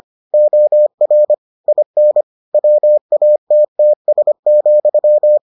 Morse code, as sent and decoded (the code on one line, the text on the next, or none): --- .-. / .. -. / .-- .- - - ... --..--
OR IN WATTS,